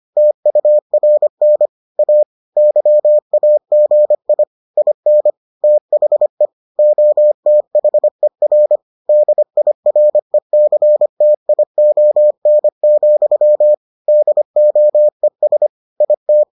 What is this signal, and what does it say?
Morse code, heard as a keyed tone, about 25 wpm, 600 Hz. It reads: TURN A YAGI IN THE OTHER DIRECTION, DOES IT